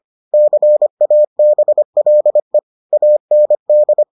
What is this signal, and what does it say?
Morse code, heard as a keyed tone, about 25 wpm, 600 Hz. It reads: CABLE AND